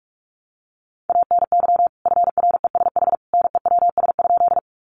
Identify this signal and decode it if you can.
Morse code, heard as a keyed tone, about 44 wpm, 700 Hz. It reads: ANY FRESH NEWS?
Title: Sentences from Top 1200 Words 45wpm